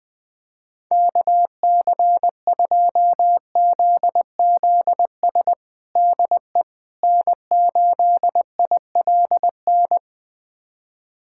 Morse code, heard as a keyed tone, about 20 wpm, 700 Hz. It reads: KC2ZZS DE N8ILN
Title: USA Callsign DE Callsign 20wpm